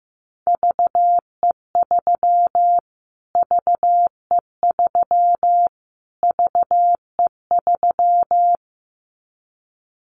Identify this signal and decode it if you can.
Morse code, heard as a keyed tone, about 14 wpm, 700 Hz. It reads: VE3 VE3 VE3
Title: Callsign Prefixes for Top 25 DXCC - Repeated 3x - 15wpm